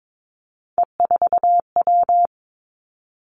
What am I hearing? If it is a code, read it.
Morse code, heard as a keyed tone, about 22 wpm, 700 Hz. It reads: E4W
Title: Three Challenging Letter-Number Combinations 22wpm